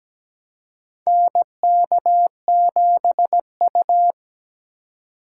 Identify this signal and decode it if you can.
Morse code, heard as a keyed tone, about 17 wpm, 700 Hz. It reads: NK7U
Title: K1USN SST QSO Practice - 17wpm